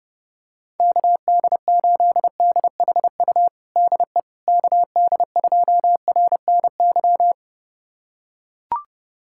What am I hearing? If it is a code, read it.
Morse code, heard as a keyed tone, about 32 wpm, 700 Hz. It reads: KD8DHU DE KD2RNY
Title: USA Callsign DE Callsign 30wpm